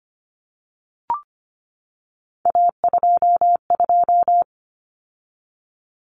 Morse code, heard as a keyed tone, about 25 wpm, 700 Hz. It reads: A22
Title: Common Cut Numbers - 3-Digit 25wpm